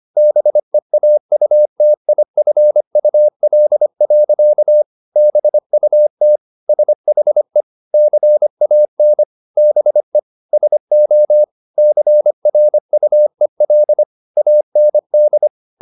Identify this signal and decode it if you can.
Morse code, heard as a keyed tone, about 25 wpm, 600 Hz. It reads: BEAUTIFUL. BUT SHE CAN BE SO CRUEL AND
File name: the old man and the sea part 1 25wpm